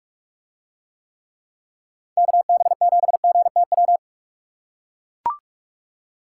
Morse code, heard as a keyed tone, about 44 wpm, 700 Hz. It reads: KB7GTW